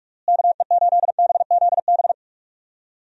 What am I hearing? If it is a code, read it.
Morse code, heard as a keyed tone, about 46 wpm, 700 Hz. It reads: KE8BZB
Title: USA Call Signs - 6 Characters - Repeated 3x - 45wpm